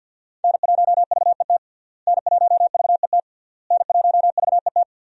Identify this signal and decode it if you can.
Morse code, heard as a keyed tone, about 51 wpm, 700 Hz. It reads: N1VET N1VET N1VET
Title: USA Call Signs - 5 Characters - Repeated 3x - 50wpm